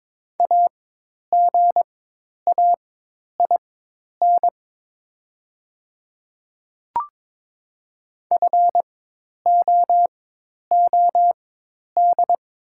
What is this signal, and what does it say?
Morse code, heard as a keyed tone, about 22 wpm, spaced as for 10 wpm, 700 Hz. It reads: AGAIN FOOD
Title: 4x Character Spacing - Top 300 Words - 22wpm